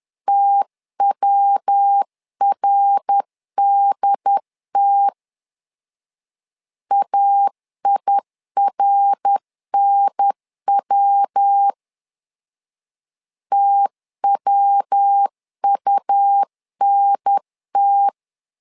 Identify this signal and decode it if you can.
Morse code, heard as a keyed tone, about 10 wpm, 800 Hz. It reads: TWRDT AIRNW TWUNT
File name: LEZIONI / 3 MWRO / 3 MWR 10wpm 1952